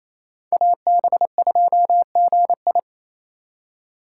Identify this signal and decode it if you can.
Morse code, heard as a keyed tone, about 28 wpm, 700 Hz. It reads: AB2GI
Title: K1USN SST QSO Practice - 28wpm